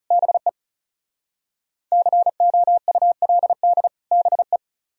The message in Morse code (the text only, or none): BE COULD BE